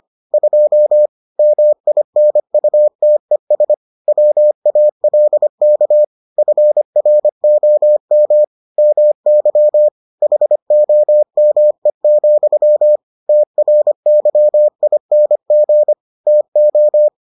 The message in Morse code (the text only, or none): 2 MINUTES WALK FROM MY HOME, TRYING TO